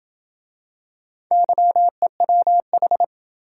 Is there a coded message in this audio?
Yes: Morse code from 1.3 to 3.0 s, heard as a keyed tone, about 27 wpm, 700 Hz. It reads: YEWH